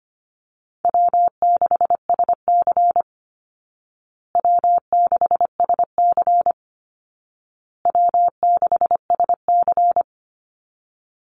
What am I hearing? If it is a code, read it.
Morse code, heard as a keyed tone, about 25 wpm, 700 Hz. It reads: W6SC W6SC W6SC
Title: Easy Call Signs Repeated 3x - 25wpm